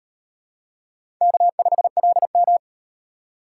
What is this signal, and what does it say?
Morse code, heard as a keyed tone, about 37 wpm, 700 Hz. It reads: K5LM